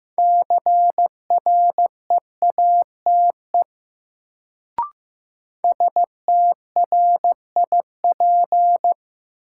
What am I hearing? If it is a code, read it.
Morse code, heard as a keyed tone, about 15 wpm, 700 Hz. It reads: CREATE STRIP
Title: Sentences from Top 1600 Words - New Word Drill Alt - 15wpm